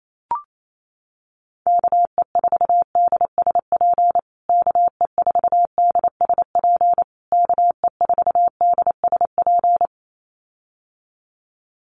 Morse code, heard as a keyed tone, about 28 wpm, 700 Hz. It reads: KE4DSP KE4DSP KE4DSP
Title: USA Call Signs Repeated 3x 28wpm